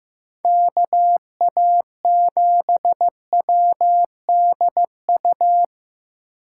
Morse code, heard as a keyed tone, about 16 wpm, 700 Hz. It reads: KA7WDU
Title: USA Call Signs - 6 Characters - 2x Character Spacing - 15wpm